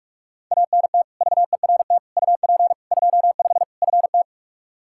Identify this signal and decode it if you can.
Morse code, heard as a keyed tone, about 42 wpm, 700 Hz. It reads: ANT VERT UP 25 FT
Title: K1USN SST QSO Practice 45wpm